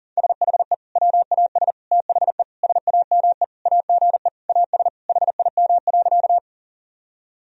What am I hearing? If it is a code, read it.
Morse code, heard as a keyed tone, about 40 wpm, 700 Hz. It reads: SHE WAS THE SAME AGE AS HIM.
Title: Sentences from Top 1000 Words - Review Part 3 - 40wpm